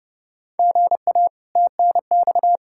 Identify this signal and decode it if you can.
Morse code, heard as a keyed tone, about 29 wpm, 700 Hz. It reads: GA TNX
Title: Practice QSO - Exchange 1 - Beginning - 30wpm